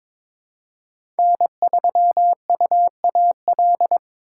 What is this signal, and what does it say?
Morse code, heard as a keyed tone, about 22 wpm, 700 Hz. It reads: N3UAL
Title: USA Call Signs - 6x Character Spacing - 22wpm